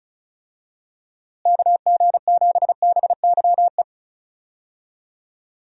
Morse code, heard as a keyed tone, about 36 wpm, 700 Hz. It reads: KG7BYE
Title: USA Call Signs - 6 Characters - Repeated 2x - 35wpm